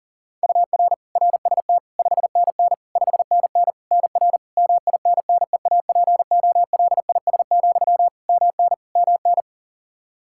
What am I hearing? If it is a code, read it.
Morse code, heard as a keyed tone, about 40 wpm, 700 Hz. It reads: UR RST 5NN 5NN NR MINNEAPOLIS, MN MN
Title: Practice QSO - Exchange 1 - Beginning - 40wpm